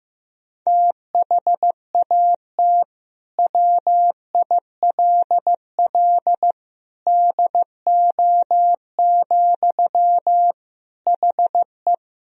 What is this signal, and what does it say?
Morse code, heard as a keyed tone, about 15 wpm, 700 Hz. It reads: THAT WILL DO, HE
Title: Sentences from Top 100 Words and Abbreviations 15wpm